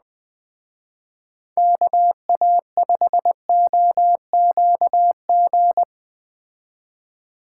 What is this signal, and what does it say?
Morse code, heard as a keyed tone, about 20 wpm, 700 Hz. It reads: KA5OQG